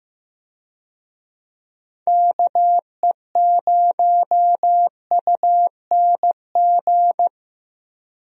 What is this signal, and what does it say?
Morse code, heard as a keyed tone, about 15 wpm, 700 Hz. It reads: KE0UNG
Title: USA Call Signs - 6 Characters - 15wpm